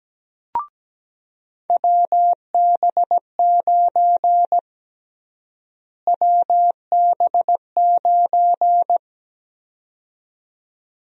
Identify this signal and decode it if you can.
Morse code, heard as a keyed tone, about 17 wpm, 700 Hz. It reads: WB9 WB9
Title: USA Call Sign Prefixes - Repeated 2x - 17wpm